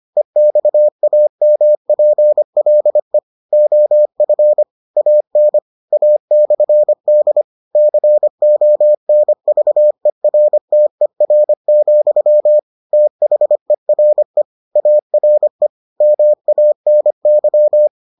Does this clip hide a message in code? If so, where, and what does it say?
Yes: Morse code from 0.2 to 17.9 s, heard as a keyed tone, about 25 wpm, 600 Hz. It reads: EXAMPLE OF AN A/D CONVERTER, THERE ARE MANY